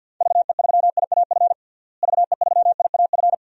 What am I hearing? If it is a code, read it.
Morse code, heard as a keyed tone, about 49 wpm, 700 Hz. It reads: VE3IAF VE3IAF
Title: DXCC-100 Call Signs - 6 Characters - Repeated 2x - 50wpm